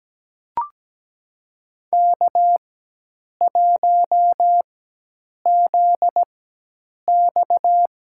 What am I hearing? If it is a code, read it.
Morse code, heard as a keyed tone, about 17 wpm, spaced as for 8 wpm, 700 Hz. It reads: K1ZX
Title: USA Call Signs - 4x Character Spacing - 17wpm